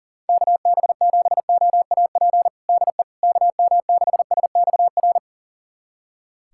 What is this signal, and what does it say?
Morse code, heard as a keyed tone, about 40 wpm, 700 Hz. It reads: KB7OAP DE KM6SXR
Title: USA Callsign DE Callsign - 6 Characters - 40wpm